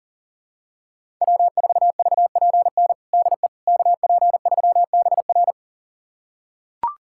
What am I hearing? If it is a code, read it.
Morse code, heard as a keyed tone, about 37 wpm, 700 Hz. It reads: W4VPN DE KP3BR
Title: USA Callsign DE Callsign - 5 Characters - 40wpm